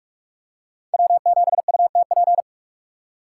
Morse code, heard as a keyed tone, about 42 wpm, 700 Hz. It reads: W7UTP